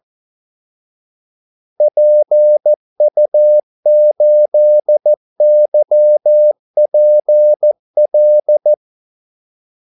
Morse code, heard as a keyed tone, about 14 wpm, 600 Hz. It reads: PU8YPL
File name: Callsigns-14wpm